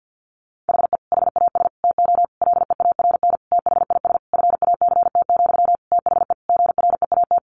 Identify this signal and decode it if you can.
Morse code, heard as a keyed tone, about 50 wpm, 700 Hz. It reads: HE HAS TO LEARN THIS FACT, THE GREAT
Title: Sentences from Top 1200 Words 50wpm